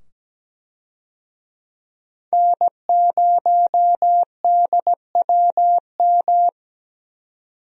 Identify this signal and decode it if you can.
Morse code, heard as a keyed tone, about 17 wpm, 700 Hz. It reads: N0DWM
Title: USA Call Signs - 5 Characters - 17wpm